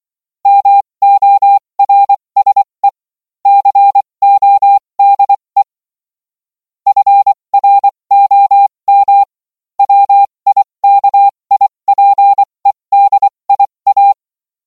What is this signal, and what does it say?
Morse code, heard as a keyed tone, about 24 wpm, spaced as for 20 wpm, 800 Hz. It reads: MORSE CODE FROM WIKIPEDIA